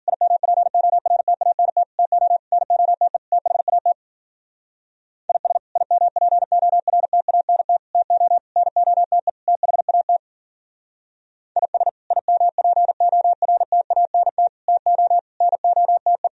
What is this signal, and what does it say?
Morse code, heard as a keyed tone, about 53 wpm, 700 Hz. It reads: IMPORTANT TO NOTE THAT IS IMPORTANT TO NOTE THAT IS IMPORTANT TO NOTE